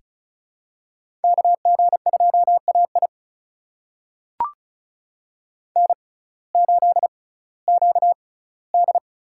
Morse code, heard as a keyed tone, about 34 wpm, spaced as for 19 wpm, 700 Hz. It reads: KG2AI N8QD
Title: USA Call Signs - 5 Characters - 6x Character Spacing - 35wpm